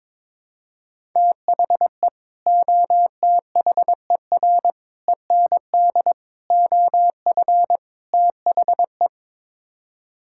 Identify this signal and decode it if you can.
Morse code, heard as a keyed tone, about 22 wpm, 700 Hz. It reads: THE OTHER END OF THE
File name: practice / 22wpm / Build A Phrase - 5-Word Phrases - 22wpm